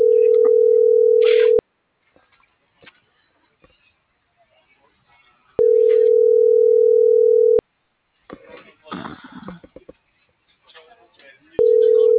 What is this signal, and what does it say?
Ringback tone (North America), 440 + 480 Hz.